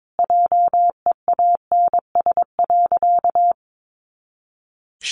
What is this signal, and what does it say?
Morse code, heard as a keyed tone, about 22 wpm, 700 Hz. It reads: JEANS.
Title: Additional Easy Phrases 22wpm